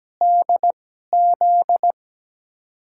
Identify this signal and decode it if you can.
Morse code, heard as a keyed tone, about 17 wpm, spaced as for 12 wpm, 700 Hz. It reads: DZ
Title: USA Call Signs - 2x Character Spacing - 17wpm